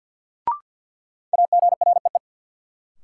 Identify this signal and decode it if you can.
Morse code, heard as a keyed tone, about 47 wpm, 700 Hz. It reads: AGREE
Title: Sets of 5 Words Encore - New Word Drill - 50wpm